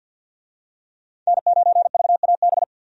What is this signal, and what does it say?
Morse code, heard as a keyed tone, about 50 wpm, 700 Hz. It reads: N9VAB
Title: USA Call Signs - 5 Characters - 50wpm